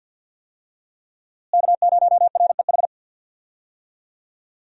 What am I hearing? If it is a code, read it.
Morse code, heard as a keyed tone, about 51 wpm, 700 Hz. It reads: K0REH
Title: USA Call Signs - 5 Characters - Repeated 2x - 50wpm